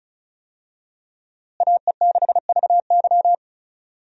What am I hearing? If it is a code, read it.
Morse code, heard as a keyed tone, about 36 wpm, 700 Hz. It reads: AE6VY
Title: USA Call Signs - 5 Characters - Repeated 3x - 35wpm